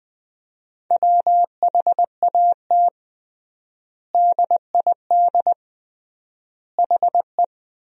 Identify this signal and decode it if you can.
Morse code, heard as a keyed tone, about 20 wpm, 700 Hz. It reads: WHAT DID HE
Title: 3x Word-Spacing - Sentences from Top 300 Words - 20wpm